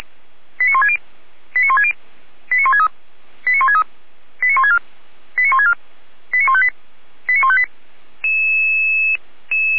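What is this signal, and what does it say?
Five-tone selective call (ZVEI), heard as quick runs of five tones, tones of about 70 ms. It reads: 89160 89160 89163 89163 89164 89164 89158 89158
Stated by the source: Received by SDR; ZVEI five-tone code 89130.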